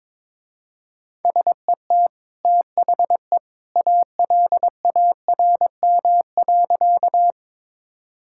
Morse code, heard as a keyed tone, about 22 wpm, 700 Hz. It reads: SET THE ALARM.
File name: practice / 22wpm / Easy Sentences with Hints - 22wpm